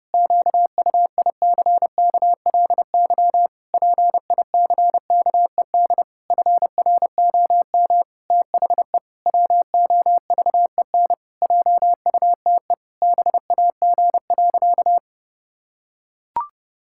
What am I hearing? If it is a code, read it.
Morse code, heard as a keyed tone, about 32 wpm, 700 Hz. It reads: QUICKLY PICKED FROM THE WOVEN JUTE BAG.
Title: Pangram Challenge v2 - 30wpm